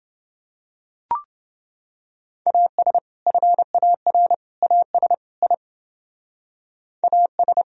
Morse code, heard as a keyed tone, about 29 wpm, 700 Hz. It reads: AS FAR AS I AS